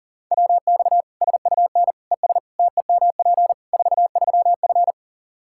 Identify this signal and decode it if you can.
Morse code, heard as a keyed tone, about 40 wpm, 700 Hz. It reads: WX SUN ES TEMP 43F
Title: Practice QSO - Exchange 2 - Info - 40wpm